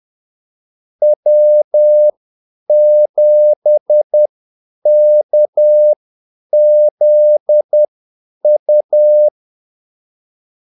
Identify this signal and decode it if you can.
Morse code, heard as a keyed tone, about 10 wpm, spaced as for 8 wpm, 600 Hz. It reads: W7KZU
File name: Callsigns-10-8wpm